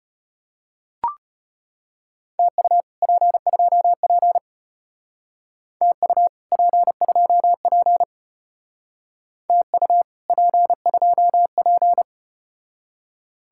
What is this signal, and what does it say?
Morse code, heard as a keyed tone, about 33 wpm, 700 Hz. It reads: TU P2P TU P2P TU P2P